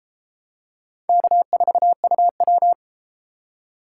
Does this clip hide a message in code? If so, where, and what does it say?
Yes: Morse code from 1.1 to 2.7 s, heard as a keyed tone, about 32 wpm, 700 Hz. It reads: K4UW